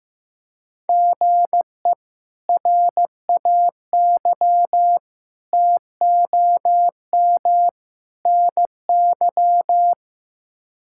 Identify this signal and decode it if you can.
Morse code, heard as a keyed tone, about 15 wpm, 700 Hz. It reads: GE RAY TOM NY
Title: K1USN SST QSO Practice - 15wpm